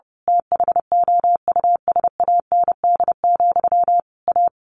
Morse code, heard as a keyed tone, about 30 wpm, 700 Hz. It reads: THOUSAND, A
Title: A Sound Of Thunder 30wpm